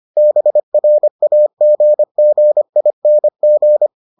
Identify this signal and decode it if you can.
Morse code, heard as a keyed tone, about 25 wpm, 600 Hz. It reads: BRAGGING